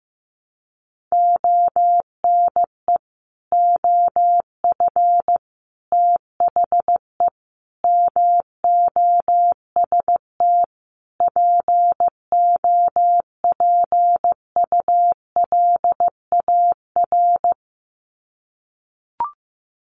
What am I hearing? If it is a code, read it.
Morse code, heard as a keyed tone, about 15 wpm, 700 Hz. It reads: ONE OF THE MOST POPULAR
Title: Sets of 5 Words 15wpm